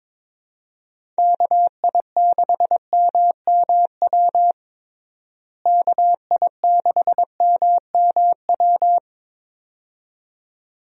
Morse code, heard as a keyed tone, about 22 wpm, 700 Hz. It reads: KI6MMW KI6MMW
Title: USA Call Signs - 6 Characters - Repeated 2x - 22wpm